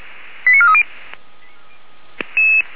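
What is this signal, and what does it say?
Five-tone selective call (ZVEI), heard as quick runs of five tones, tones of about 70 ms. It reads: 89430
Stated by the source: ZVEI five-tone selective call received via SDR.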